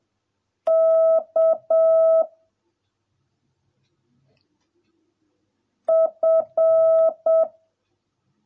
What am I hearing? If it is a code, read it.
Morse code, heard as a keyed tone, about 7 wpm, 650 Hz. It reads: K F